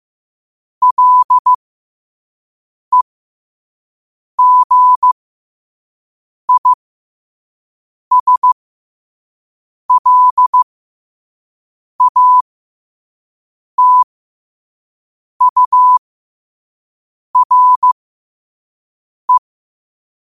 Morse code, heard as a keyed tone, about 15 wpm, spaced as for 5 wpm, 1000 Hz. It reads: LEGISLATURE
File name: qotdfiles / 5wpm